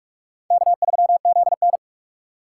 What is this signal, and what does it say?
Morse code, heard as a keyed tone, about 56 wpm, 700 Hz. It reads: K3ZN
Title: Speed Racing - USA 1x2 Call Signs 45wpm